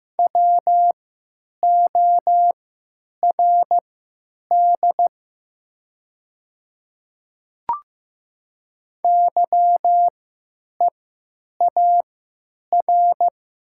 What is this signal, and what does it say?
Morse code, heard as a keyed tone, about 15 wpm, spaced as for 9 wpm, 700 Hz. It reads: WORD YEAR
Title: 3x Character Spacing - Top 200 Words - 15wpm